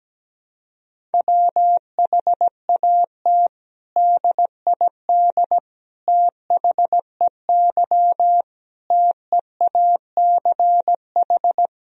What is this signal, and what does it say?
Morse code, heard as a keyed tone, about 17 wpm, 700 Hz. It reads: WHAT DID THEY TEACH
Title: Sentences from Top 1000 Words - Review Part 2 - 17wpm